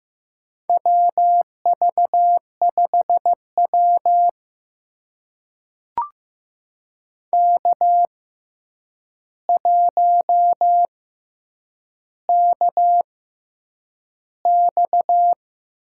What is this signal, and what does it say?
Morse code, heard as a keyed tone, about 15 wpm, spaced as for 8 wpm, 700 Hz. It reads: WV5W K1KX
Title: USA Call Signs - 4 Characters - 6x Character Spacing - 15wpm